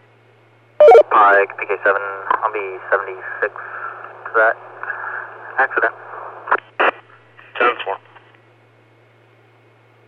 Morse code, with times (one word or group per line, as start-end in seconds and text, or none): none